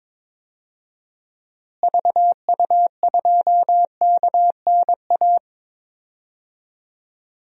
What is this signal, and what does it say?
Morse code, heard as a keyed tone, about 22 wpm, 700 Hz. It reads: VU2KNA
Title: DXCC-100 Call Signs - 6 Characters - Repeated 2x - 22wpm